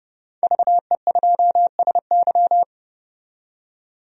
Morse code, heard as a keyed tone, about 32 wpm, 700 Hz. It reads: VE2SY
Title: DXCC-100 Call Signs - 5 Characters - 30wpm